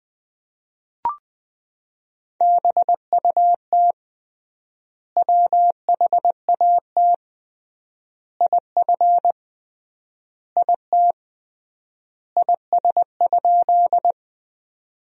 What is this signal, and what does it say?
Morse code, heard as a keyed tone, about 20 wpm, 700 Hz. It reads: BUT WHAT IF IT IS?
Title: 3x Word-Spacing - Sentences from Top 100 Words - 20wpm